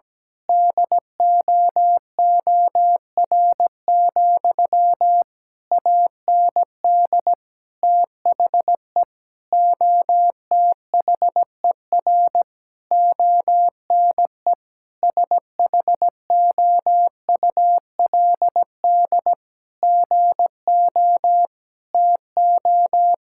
Morse code, heard as a keyed tone, about 17 wpm, 700 Hz. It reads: DOOR, AND THE OTHER ONE SHOULD GO TO
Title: Sentences from Top 1600 Words 17wpm